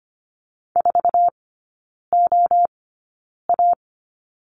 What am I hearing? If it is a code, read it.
Morse code, heard as a keyed tone, about 25 wpm, spaced as for 9 wpm, 700 Hz. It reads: 4OA
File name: Lesson 17e - Letter u - Three Characters 25wpm